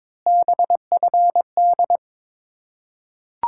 Morse code, heard as a keyed tone, about 22 wpm, 700 Hz. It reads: BFD
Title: Ontario QSO Party Exchange 22wpm